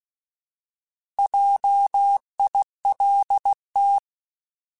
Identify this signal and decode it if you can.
Morse code, heard as a keyed tone, about 16 wpm, 800 Hz. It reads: JILT
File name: Alphabetical / Alphabetical mp3s / Js @15